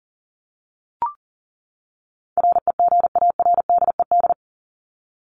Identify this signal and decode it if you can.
Morse code, heard as a keyed tone, about 40 wpm, 700 Hz. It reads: REGARDED
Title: ED Suffix - Top 100 Words with Base Words 40wpm